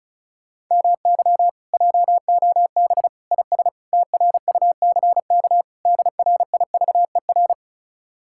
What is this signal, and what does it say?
Morse code, heard as a keyed tone, about 46 wpm, 700 Hz. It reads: MY JOB IS TRUCK DRIVER